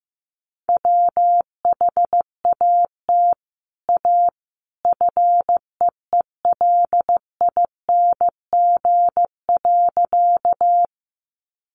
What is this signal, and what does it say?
Morse code, heard as a keyed tone, about 15 wpm, 700 Hz. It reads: WHAT A FEELING.
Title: Sentences from Top 1200 Words 15wpm